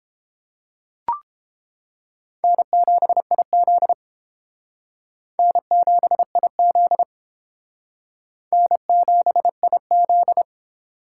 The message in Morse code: N7IZ N7IZ N7IZ

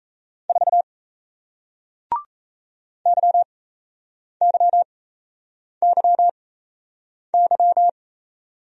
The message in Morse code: 4 YYYY

